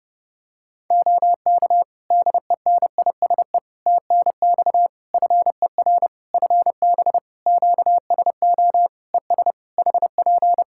Morse code, heard as a keyed tone, about 30 wpm, 700 Hz. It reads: OK DENISE TNX FER FB QSO ES HP
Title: Practice QSO - Exchange 4 - Ending - 30wpm